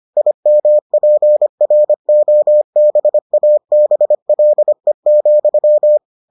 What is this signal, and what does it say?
Morse code, heard as a keyed tone, about 25 wpm, 600 Hz. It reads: IMPROBABLE,